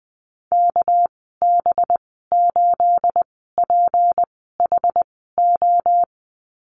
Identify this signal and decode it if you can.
Morse code, heard as a keyed tone, about 20 wpm, spaced as for 15 wpm, 700 Hz. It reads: KB8PHO
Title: 213 Lesson 34g - Number 8 - Call Signs 20wpm